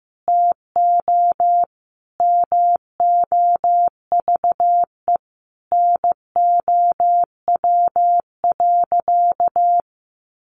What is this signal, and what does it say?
Morse code, heard as a keyed tone, about 14 wpm, 700 Hz. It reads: TO MOVE NOW.
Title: Sentences from Top 1200 Words 15wpm